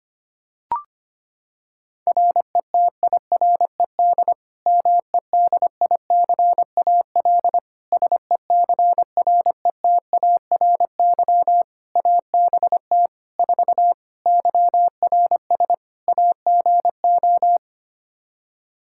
Morse code, heard as a keyed tone, about 25 wpm, 700 Hz. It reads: RETIRED MEDICAL SECRETARY ABT 4 YRS AGO
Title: Practice QSO - Exchange 3 - More Info - 25wpm